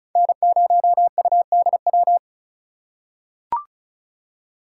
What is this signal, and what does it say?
Morse code, heard as a keyed tone, about 36 wpm, 700 Hz. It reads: N0UDW
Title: USA Call Signs - 5 Characters - 2x Character Spacing - 35wpm